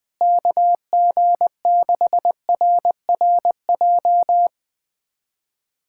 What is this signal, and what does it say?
Morse code, heard as a keyed tone, about 21 wpm, 700 Hz. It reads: KG6RRJ
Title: USA Callsign DE Callsign - 6 Characters - 20wpm